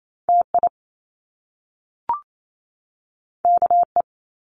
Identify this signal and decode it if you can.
Morse code, heard as a keyed tone, about 28 wpm, 700 Hz. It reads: TI KE
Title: Top Two Letter Combinations in Words 28wpm